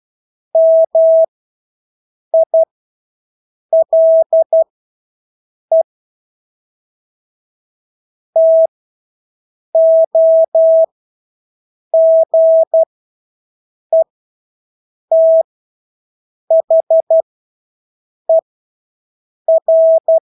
Morse code, heard as a keyed tone, about 12 wpm, spaced as for 6 wpm, 650 Hz. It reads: MILE TOGETHER
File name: Engelske ord 12-6 wpm